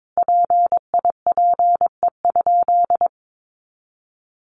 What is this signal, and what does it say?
Morse code, heard as a keyed tone, about 22 wpm, 700 Hz. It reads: PIPE?
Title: Sentences from Top 1500 Words 22wpm